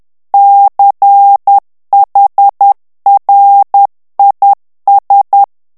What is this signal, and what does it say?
Morse code, heard as a keyed tone, about 10 wpm, 800 Hz. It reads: CHRIS
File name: ANGLAIS texte clair 3 vitesse 10 mots minute